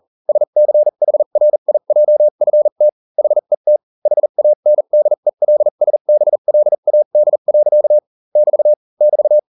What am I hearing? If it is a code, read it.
Morse code, heard as a keyed tone, about 40 wpm, spaced as for 35 wpm, 600 Hz. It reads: SCHRIJFT HET HANDELSBLAD. <BT> <BT>